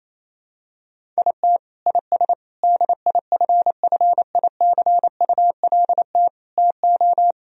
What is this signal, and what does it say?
Morse code, heard as a keyed tone, about 28 wpm, 700 Hz. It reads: IT IS DIFFICULT TO
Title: Sentences from Top 1000 Words - Review Part 3 - 28wpm